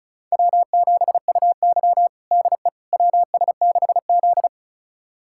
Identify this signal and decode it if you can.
Morse code, heard as a keyed tone, about 34 wpm, 700 Hz. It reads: W7UY DE WS6Z
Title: USA Callsign DE Callsign - 4 Characters - 35wpm